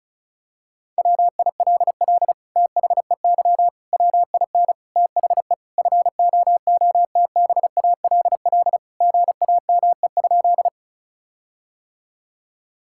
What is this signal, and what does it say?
Morse code, heard as a keyed tone, about 36 wpm, 700 Hz. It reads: WILL THEY WIN THE FOOTBALL GAME?
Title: Sentences from Top 1600 Words 35wpm